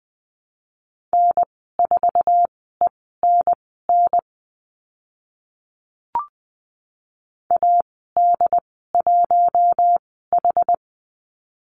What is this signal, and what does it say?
Morse code, heard as a keyed tone, about 20 wpm, spaced as for 14 wpm, 700 Hz. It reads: N4ENN AD1H
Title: 069 Lesson 12g - Letter d - Call Signs 20wpm